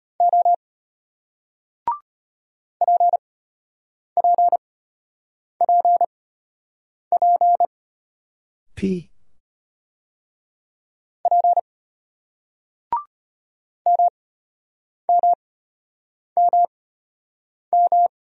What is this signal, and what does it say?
Morse code, heard as a keyed tone, about 35 wpm, spaced as for 8 wpm, 700 Hz. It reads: O PPPP P MMMM